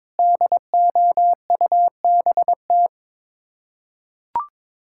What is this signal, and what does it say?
Morse code, heard as a keyed tone, about 22 wpm, 700 Hz. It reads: DOUBT
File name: practice / 22wpm / Sets of 7 Words - 22wpm